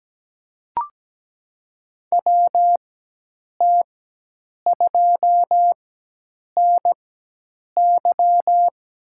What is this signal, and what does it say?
Morse code, heard as a keyed tone, about 17 wpm, spaced as for 8 wpm, 700 Hz. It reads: WT2NY
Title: USA Call Signs - 5 Characters - 4x Character Spacing - 17wpm